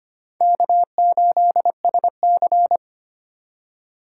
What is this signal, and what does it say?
Morse code, heard as a keyed tone, about 25 wpm, 700 Hz. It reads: K8SC
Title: USA Call Signs - 4 Characters - 3x Character Spacing - 25wpm